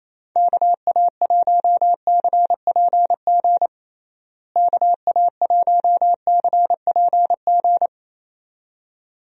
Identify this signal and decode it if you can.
Morse code, heard as a keyed tone, about 28 wpm, 700 Hz. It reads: KA1CPG KA1CPG
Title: USA Call Signs - 6 Characters - Repeated 2x - 28wpm